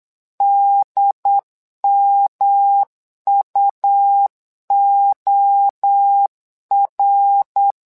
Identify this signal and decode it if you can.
Morse code, heard as a keyed tone, about 8 wpm, 800 Hz. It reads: DMUOR
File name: LEZIONI / 3 MWRO / 3 MWRO 8wpm 2606